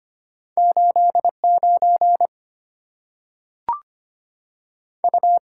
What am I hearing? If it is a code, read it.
Morse code, heard as a keyed tone, about 25 wpm, 700 Hz. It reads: 8 9 U
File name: Two Letter-Number Combinations - 25wpm